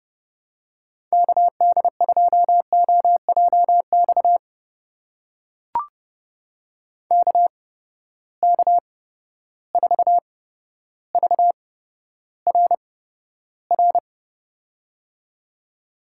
Morse code, heard as a keyed tone, about 30 wpm, spaced as for 13 wpm, 700 Hz. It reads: KD2OJX KK4VRR
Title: USA Call Signs - 6 Characters - 8x Character Spacing - 30wpm